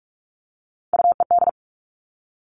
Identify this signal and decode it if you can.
Morse code, heard as a keyed tone, about 46 wpm, 700 Hz. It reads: UED
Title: Three Challenging Letter-Number Combinations 45wpm